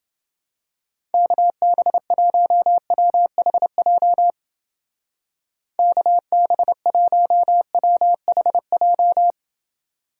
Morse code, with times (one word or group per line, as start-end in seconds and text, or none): 1.1-4.3 KB1WHJ
5.8-9.3 KB1WHJ